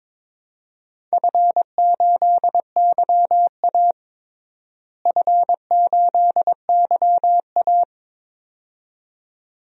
Morse code, heard as a keyed tone, about 22 wpm, 700 Hz. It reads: F8YA F8YA
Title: DXCC-100 Call Signs - 4 Characters - Repeated 2x - 22wpm